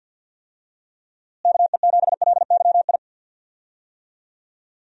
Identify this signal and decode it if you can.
Morse code, heard as a keyed tone, about 52 wpm, 700 Hz. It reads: KE7LYI
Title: USA Call Signs - 6 Characters - Repeated 2x - 50wpm